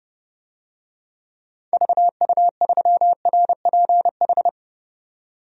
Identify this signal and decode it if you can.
Morse code, heard as a keyed tone, about 31 wpm, 700 Hz. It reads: VU3RPH
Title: DXCC-100 Call Signs - 6 Characters - 30wpm